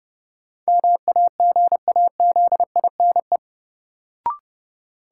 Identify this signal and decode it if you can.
Morse code, heard as a keyed tone, about 29 wpm, 700 Hz. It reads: MAGAZINE
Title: Sentences from Top 1000 Words - Review Part 7 - New Word Drill - 30wpm